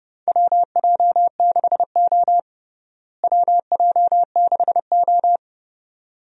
Morse code, heard as a keyed tone, about 30 wpm, 700 Hz. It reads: WJ6O WJ6O
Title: USA Call Signs - 4 Characters - Repeated 2x - 30wpm